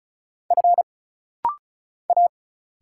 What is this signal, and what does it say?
Morse code, heard as a keyed tone, about 37 wpm, 700 Hz. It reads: F A
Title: Single Letters - Rapid-Fire 35wpm